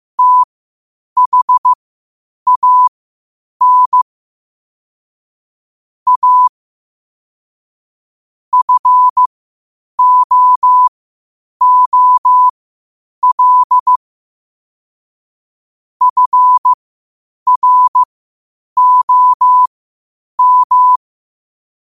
Morse code, heard as a keyed tone, about 15 wpm, spaced as for 8 wpm, 1000 Hz. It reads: THAN A FOOL FROM
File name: qotdfiles / 8wpm